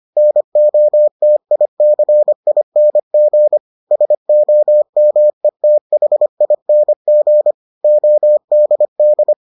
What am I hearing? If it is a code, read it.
Morse code, heard as a keyed tone, about 25 wpm, 600 Hz. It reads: NOTICING SOMETHING ODD